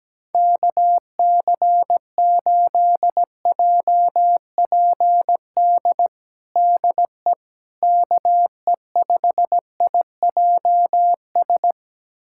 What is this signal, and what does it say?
Morse code, heard as a keyed tone, about 17 wpm, 700 Hz. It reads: KC8JPD DE KE5IJS
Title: USA Callsign DE Callsign - 6 Characters - 17wpm